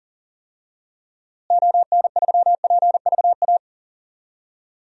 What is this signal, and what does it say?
Morse code, heard as a keyed tone, about 40 wpm, 700 Hz. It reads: ON3PVA
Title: DXCC-100 Call Signs - 6 Characters - Repeated 2x - 40wpm